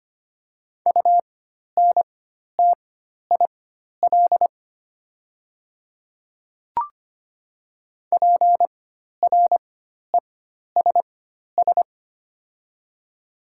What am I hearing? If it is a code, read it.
Morse code, heard as a keyed tone, about 25 wpm, spaced as for 12 wpm, 700 Hz. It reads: UNTIL PRESS
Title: 4x Character Spacing - Top 300 Words - 25wpm